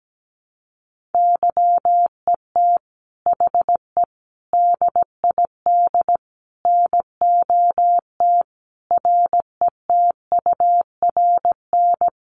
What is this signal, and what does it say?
Morse code, heard as a keyed tone, about 17 wpm, 700 Hz. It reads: YET HE DID NOT RETURN
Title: Sentences from Top 800 Words 17wpm